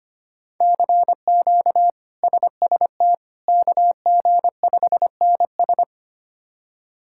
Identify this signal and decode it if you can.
Morse code, heard as a keyed tone, about 25 wpm, 700 Hz. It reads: CQ SST KG5NS
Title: K1USN SST QSO Practice - 25wpm